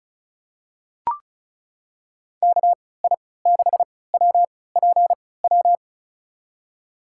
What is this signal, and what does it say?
Morse code, heard as a keyed tone, about 34 wpm, spaced as for 20 wpm, 700 Hz. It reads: KI6WPW
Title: USA Call Signs - 6 Characters - 3x Character Spacing - 35wpm